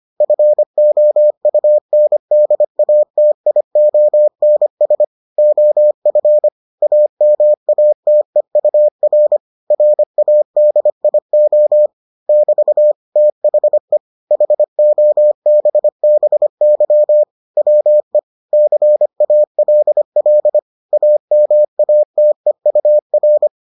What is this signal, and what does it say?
Morse code, heard as a keyed tone, about 25 wpm, 600 Hz. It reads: FOUNDATIONS OF AMATEUR RADIO <BT> THE HOBBY WE CALL AMATEUR